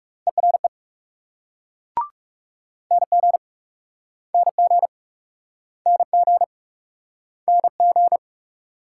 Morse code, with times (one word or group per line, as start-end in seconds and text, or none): 0.3-0.7 ERE
2.9-3.4 NG
4.3-4.9 NG
5.9-6.4 NG
7.5-8.2 NG